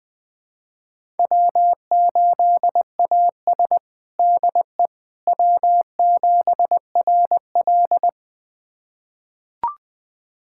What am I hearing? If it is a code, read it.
Morse code, heard as a keyed tone, about 20 wpm, 700 Hz. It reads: W8AS DE W7RL
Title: USA Callsign DE Callsign - 4 Characters - 20wpm